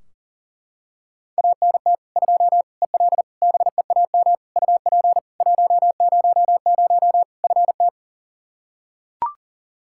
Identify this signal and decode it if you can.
Morse code, heard as a keyed tone, about 40 wpm, 700 Hz. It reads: ANT 2 EL BEAM UP 100 FT
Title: Practice QSO - Exchange 2 - Info - 40wpm